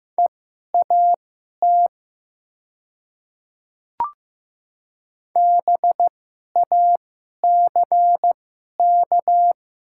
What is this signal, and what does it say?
Morse code, heard as a keyed tone, about 15 wpm, spaced as for 11 wpm, 700 Hz. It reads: EAT BACK
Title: 2x Character Spacing - Top 300 Words - 15wpm